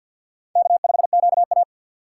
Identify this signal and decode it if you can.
Morse code, heard as a keyed tone, about 49 wpm, 700 Hz. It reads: K5QA
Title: USA Call Signs - 4 Characters - 2x Character Spacing - 50wpm